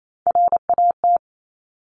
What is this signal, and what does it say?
Morse code, heard as a keyed tone, about 28 wpm, 700 Hz. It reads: RAT
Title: Top Three Letter Combinations in Words 28wpm